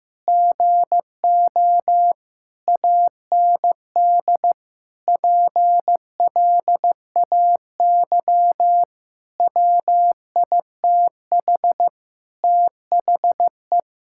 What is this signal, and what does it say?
Morse code, heard as a keyed tone, about 15 wpm, 700 Hz. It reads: GO AND PLAY WITH THE
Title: Sentences from Top 1600 Words 15wpm